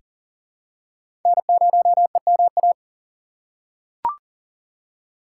Morse code, heard as a keyed tone, about 40 wpm, 700 Hz. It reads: N0EMA
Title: USA Call Signs - 5 Characters - 3x Character Spacing - 40wpm